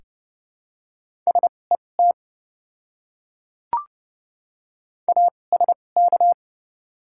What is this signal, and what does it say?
Morse code, heard as a keyed tone, about 30 wpm, spaced as for 22 wpm, 700 Hz. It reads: SET ASK
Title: 2x Character Spacing - Top 200 Words - 30wpm